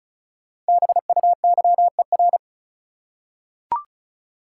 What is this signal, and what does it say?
Morse code, heard as a keyed tone, about 34 wpm, 700 Hz. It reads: BUYER
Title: Five Letter Words - 35wpm